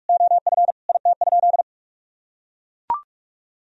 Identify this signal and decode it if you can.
Morse code, heard as a keyed tone, about 41 wpm, 700 Hz. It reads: OF IT?
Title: Sentences from Top 1000 Words - Review Part 4 - 45wpm